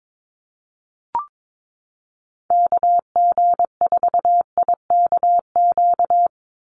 Morse code, heard as a keyed tone, about 22 wpm, 700 Hz. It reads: KG4IKQ
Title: Call Signs Repeated 2x - 22wpm